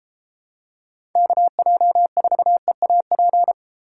Morse code, heard as a keyed tone, about 31 wpm, 700 Hz. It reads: KJ4EAP